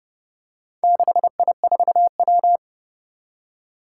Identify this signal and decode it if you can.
Morse code, heard as a keyed tone, about 30 wpm, 700 Hz. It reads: 6I4W
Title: Four Challenging Letter-Number Combinations 30wpm